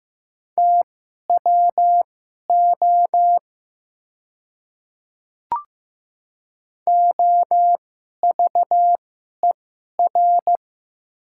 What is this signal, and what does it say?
Morse code, heard as a keyed tone, about 15 wpm, spaced as for 11 wpm, 700 Hz. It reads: TWO OVER
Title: 2x Character Spacing - Top 200 Words - 15wpm